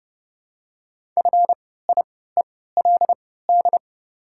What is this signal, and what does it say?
Morse code, heard as a keyed tone, about 30 wpm, spaced as for 17 wpm, 700 Hz. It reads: FIELD
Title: 3x Character Spacing - Top 500 Words - 30wpm